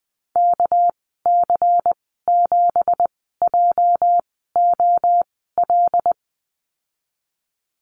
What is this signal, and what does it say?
Morse code, heard as a keyed tone, about 20 wpm, spaced as for 14 wpm, 700 Hz. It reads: KC7JOL
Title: 238 Lesson 38g - Letter q - Call Signs 20wpm